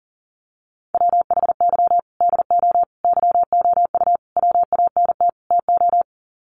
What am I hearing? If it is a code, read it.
Morse code, heard as a keyed tone, about 37 wpm, 700 Hz. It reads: WHY DO YOU WANT TO
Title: Sets of 5 Words Encore2 40wpm